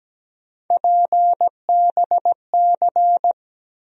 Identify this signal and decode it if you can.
Morse code, heard as a keyed tone, about 17 wpm, 700 Hz. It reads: PBC
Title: USA Call Sign Postfixes - 3 Letters 17wpm